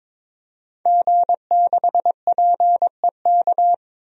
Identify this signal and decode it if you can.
Morse code, heard as a keyed tone, about 22 wpm, 700 Hz. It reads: G6PEK
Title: DXCC-100 Call Signs - 5 Characters - 22wpm